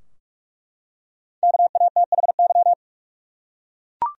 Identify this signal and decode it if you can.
Morse code, heard as a keyed tone, about 42 wpm, 700 Hz. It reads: KATHY
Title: USA Common Names - 45wpm